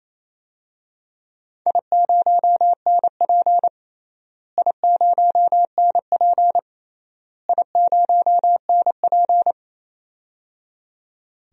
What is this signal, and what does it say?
Morse code, heard as a keyed tone, about 28 wpm, 700 Hz. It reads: I0NP I0NP I0NP
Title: DXCC-100 Call Signs - 4 Characters - Repeated 3x - 28wpm